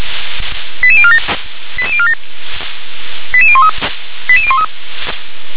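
Five-tone selective call (ZVEI), heard as quick runs of five tones, tones of about 70 ms. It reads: 88047 88047 88013 88013
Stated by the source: ZVEI five-tone call received by SDR.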